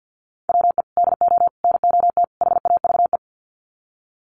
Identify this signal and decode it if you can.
Morse code, heard as a keyed tone, about 50 wpm, 700 Hz. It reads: WE DO NOT HAVE
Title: Sets of 4 Words 50wpm